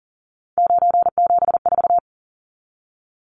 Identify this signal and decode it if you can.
Morse code, heard as a keyed tone, about 40 wpm, 700 Hz. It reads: 974
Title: Numbers - 3-Digit 40wpm